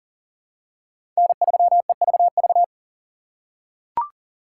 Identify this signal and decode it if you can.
Morse code, heard as a keyed tone, about 40 wpm, 700 Hz. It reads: N3EVV
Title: USA Call Signs - 5 Characters - 2x Character Spacing - 40wpm